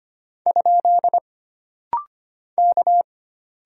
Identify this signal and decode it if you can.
Morse code, heard as a keyed tone, about 25 wpm, 700 Hz. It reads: ? K
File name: Single Character (Letter:Number:Punctuation:Prosign) - Rapid-Fire - 25wpm